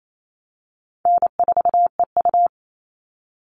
Morse code, heard as a keyed tone, about 28 wpm, 700 Hz. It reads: N4EU
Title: All Cut Numbers - 4-Digit 28wpm